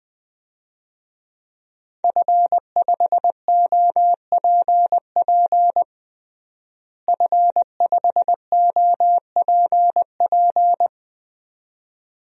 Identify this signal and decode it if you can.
Morse code, heard as a keyed tone, about 20 wpm, 700 Hz. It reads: F5OPP F5OPP
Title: DXCC-100 Call Signs - 5 Characters - Repeated 2x - 20wpm